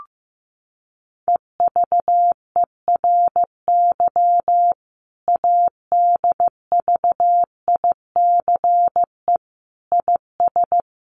Morse code, heard as a keyed tone, about 15 wpm, 700 Hz. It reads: EVERY ADVICE IS
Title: Sentences from Top 1100 Words 15wpm